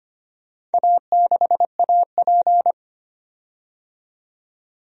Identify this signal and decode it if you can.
Morse code, heard as a keyed tone, about 25 wpm, 700 Hz. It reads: A6AP
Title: DXCC-100 Call Signs - 4 Characters - 25wpm